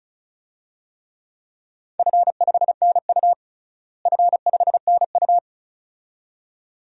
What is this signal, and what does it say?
Morse code, heard as a keyed tone, about 36 wpm, 700 Hz. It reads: F5NU F5NU
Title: DXCC-100 Call Signs - 4 Characters - Repeated 2x - 35wpm